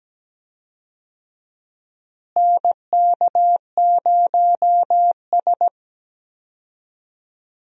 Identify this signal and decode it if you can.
Morse code, heard as a keyed tone, about 17 wpm, 700 Hz. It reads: NK0S
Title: USA Call Signs - 4 Characters - Repeated 3x - 17wpm